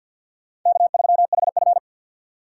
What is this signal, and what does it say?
Morse code, heard as a keyed tone, about 64 wpm, 700 Hz. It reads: K3HF